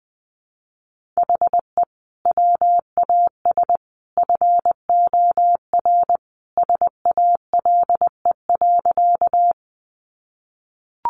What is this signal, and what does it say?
Morse code, heard as a keyed tone, about 20 wpm, 700 Hz. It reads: HE WAS FOR SALE.
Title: Sentences from Top 1400 Words 20wpm